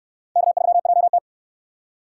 Morse code, heard as a keyed tone, about 62 wpm, 700 Hz. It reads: K3JT